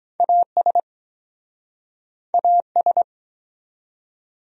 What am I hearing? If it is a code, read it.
Morse code, heard as a keyed tone, about 24 wpm, 700 Hz. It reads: AS AS